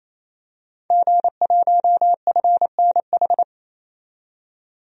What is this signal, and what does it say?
Morse code, heard as a keyed tone, about 27 wpm, 700 Hz. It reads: G1FNH
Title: DXCC-100 Call Signs - 5 Characters - 28wpm